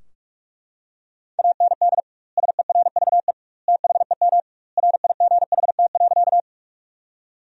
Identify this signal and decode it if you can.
Morse code, heard as a keyed tone, about 41 wpm, 700 Hz. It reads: AND SERVE THEM RIGHT.
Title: 2x Word-Spacing - Sentences from Top 400 Words - 45wpm